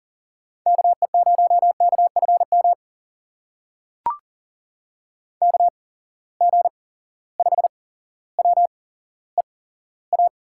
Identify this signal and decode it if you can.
Morse code, heard as a keyed tone, about 40 wpm, spaced as for 17 wpm, 700 Hz. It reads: KE0KFM KG5WEA
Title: USA Call Signs - 6 Characters - 8x Character Spacing - 40wpm